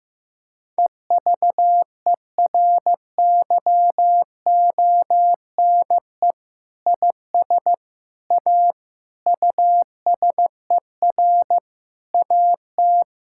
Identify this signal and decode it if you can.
Morse code, heard as a keyed tone, about 16 wpm, 700 Hz. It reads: EVERYONE IS A USER AT
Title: Sentences from Top 1000 Words - Review Part 5 - 15wpm